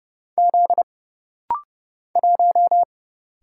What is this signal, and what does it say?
Morse code, heard as a keyed tone, about 30 wpm, 700 Hz. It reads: Z 1
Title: Single Letter-Number - Rapid-Fire 30wpm